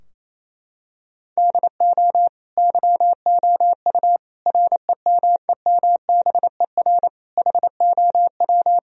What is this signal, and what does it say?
Morse code, heard as a keyed tone, about 28 wpm, 700 Hz. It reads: DO YOU REMEMBER HOW
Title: Sentences from Top 1000 Words - Review Part 3 - 28wpm